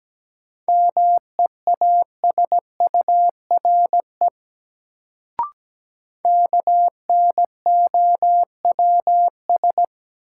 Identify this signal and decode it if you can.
Morse code, heard as a keyed tone, about 17 wpm, 700 Hz. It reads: MEASURE KNOWS
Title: Sentences from Top 1000 Words - Review Part 1 - New Word Drill - 17wpm